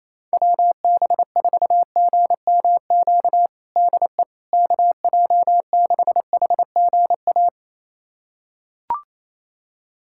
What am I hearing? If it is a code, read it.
Morse code, heard as a keyed tone, about 28 wpm, 700 Hz. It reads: WB4GMQ DE KJ6HGA
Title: USA Callsign DE Callsign - 6 Characters - 28wpm